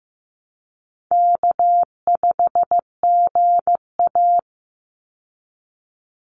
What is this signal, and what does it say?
Morse code, heard as a keyed tone, about 15 wpm, 700 Hz. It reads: K5GA
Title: Easy Call Signs Repeated 2x - 15wpm